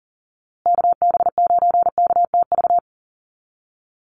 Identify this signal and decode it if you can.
Morse code, heard as a keyed tone, about 40 wpm, 700 Hz. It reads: KB9KTV
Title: USA Call Signs Repeated 2x 40wpm